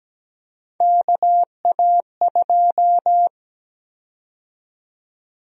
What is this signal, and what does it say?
Morse code, heard as a keyed tone, about 17 wpm, 700 Hz. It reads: KA2
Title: USA Call Sign Prefixes - 17wpm